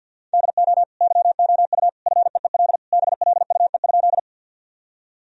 Morse code, heard as a keyed tone, about 50 wpm, 700 Hz. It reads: DO YOU FEEL BLUE?
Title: Sentences from Top 1000 Words - Review Part 3 - 50wpm